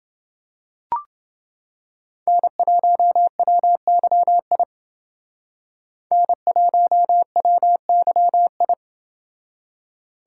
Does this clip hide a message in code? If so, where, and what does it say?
Yes: Morse code from 2.3 to 8.7 s, heard as a keyed tone, about 29 wpm, 700 Hz. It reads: N1WYI N1WYI